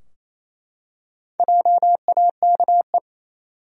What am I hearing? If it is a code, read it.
Morse code, heard as a keyed tone, about 28 wpm, 700 Hz. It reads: JAKE
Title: USA Common Names - Max 4 Letters - 28wpm